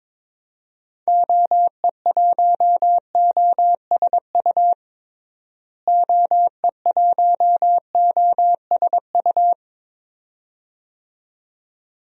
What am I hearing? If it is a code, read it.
Morse code, heard as a keyed tone, about 22 wpm, 700 Hz. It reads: OE1OSU OE1OSU
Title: DXCC-100 Call Signs - 6 Characters - Repeated 2x - 22wpm